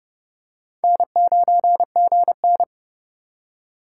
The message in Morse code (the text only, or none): N9GN